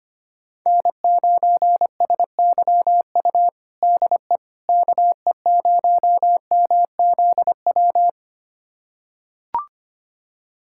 Morse code, heard as a keyed tone, about 25 wpm, 700 Hz. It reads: N9SYU DE KE0MZW
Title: USA Callsign DE Callsign 25wpm